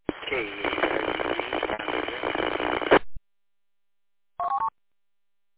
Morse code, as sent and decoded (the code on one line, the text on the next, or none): none